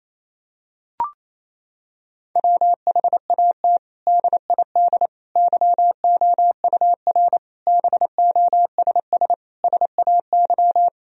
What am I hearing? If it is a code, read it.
Morse code, heard as a keyed tone, about 27 wpm, 700 Hz. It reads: WHAT DID YOUR BOSS SAY
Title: Sentences from Top 1000 Words - Review Part 4 - 28wpm